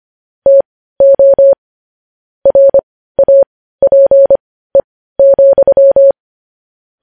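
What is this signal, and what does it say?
Morse code, heard as a keyed tone, about 25 wpm, spaced as for 15 wpm, 550 Hz. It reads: TO RAPE,
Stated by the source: Morse news headlines practice at 15.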